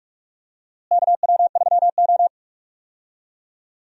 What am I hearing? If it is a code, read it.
Morse code, heard as a keyed tone, about 42 wpm, 700 Hz. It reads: KW3O
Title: USA Call Signs - 4 Characters - Repeated 3x - 45wpm